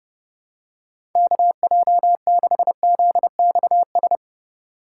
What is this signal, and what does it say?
Morse code, heard as a keyed tone, about 30 wpm, 700 Hz. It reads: KJ6ZXS
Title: USA Call Signs - 6 Characters - 4x Character Spacing - 30wpm